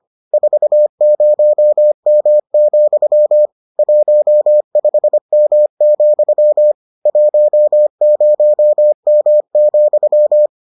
Morse code, heard as a keyed tone, about 25 wpm, 600 Hz. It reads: 40M, 15M, 10M,